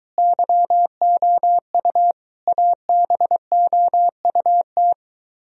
Morse code, heard as a keyed tone, about 23 wpm, 700 Hz. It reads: YOU ABOUT